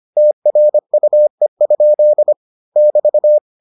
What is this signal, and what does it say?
Morse code, heard as a keyed tone, about 25 wpm, 600 Hz. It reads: TRUE? <BT>